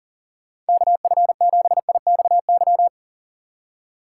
Morse code, heard as a keyed tone, about 52 wpm, 700 Hz. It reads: KF7IXY